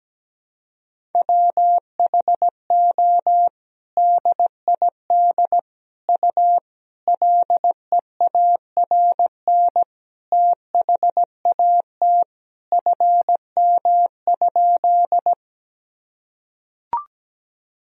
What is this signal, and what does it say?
Morse code, heard as a keyed tone, about 17 wpm, 700 Hz. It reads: WHO DID U LEARN THAT FM?
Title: Sentences from Top 300 Words and Abbreviations 17wpm